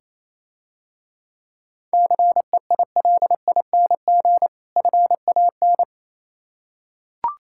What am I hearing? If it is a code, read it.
Morse code, heard as a keyed tone, about 27 wpm, 690 Hz. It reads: CEILING FAN
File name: Easy Words - Speed-Racing with Hints - 22wpm